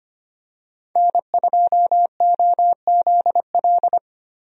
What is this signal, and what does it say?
Morse code, heard as a keyed tone, about 25 wpm, 700 Hz. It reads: N2OZL
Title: USA Call Signs - 5 Characters - Repeated 3x - 25wpm